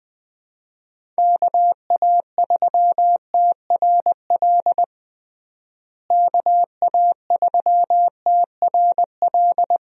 Morse code, heard as a keyed tone, about 21 wpm, 700 Hz. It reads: KA3TRL KA3TRL
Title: USA Call Signs - 6 Characters - Repeated 3x - 20wpm